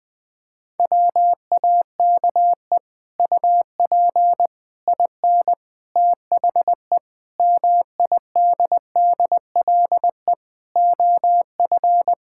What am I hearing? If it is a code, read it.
Morse code, heard as a keyed tone, about 20 wpm, 700 Hz. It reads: WAKE UP IN THE MIDDLE OF
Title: Sets of 8 Words - 20wpm